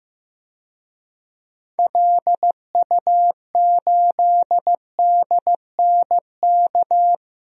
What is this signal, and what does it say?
Morse code, heard as a keyed tone, about 15 wpm, 700 Hz. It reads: LU8DNK
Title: DXCC-100 Call Signs - 6 Characters - 15wpm